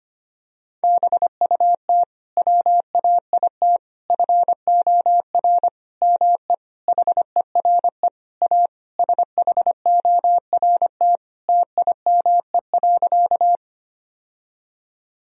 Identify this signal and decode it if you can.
Morse code, heard as a keyed tone, about 25 wpm, 700 Hz. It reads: BUT WAIT FOR ME HERE A SHORT TIME.
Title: Sentences from Top 1000 Words - Review Part 2 - 25wpm